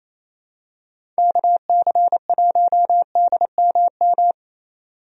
Morse code, heard as a keyed tone, about 27 wpm, 700 Hz. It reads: KC1DMM
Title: USA Call Signs - 6 Characters - 28wpm